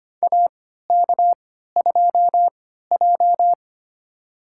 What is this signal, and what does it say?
Morse code, heard as a keyed tone, about 25 wpm, spaced as for 14 wpm, 700 Hz. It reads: AK2J
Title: USA Call Signs - 3x Character Spacing - 25wpm